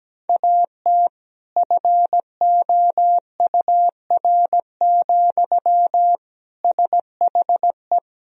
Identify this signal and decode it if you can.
Morse code, heard as a keyed tone, about 17 wpm, 700 Hz. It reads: AT FOUR, SHE
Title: Sentences from Top 1000 Words - Review Part 1 - 17wpm